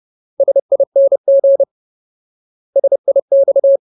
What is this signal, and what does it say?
Morse code, heard as a keyed tone, about 30 wpm, 550 Hz. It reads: SING SIX
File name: Ultimate Word List / 30WPM Ultimate